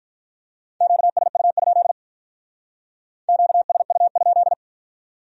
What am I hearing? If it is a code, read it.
Morse code, heard as a keyed tone, about 44 wpm, 700 Hz. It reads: QSU? QSU?